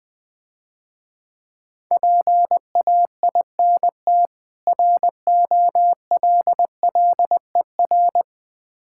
Morse code, heard as a keyed tone, about 20 wpm, 700 Hz. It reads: PAINT ROLLER
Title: Difficult Words - Speed-Racing with Hints - 20wpm